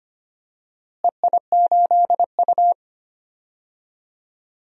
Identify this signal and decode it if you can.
Morse code, heard as a keyed tone, about 25 wpm, 700 Hz. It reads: EI8U
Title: DXCC-100 Call Signs - 4 Characters - Repeated 2x - 25wpm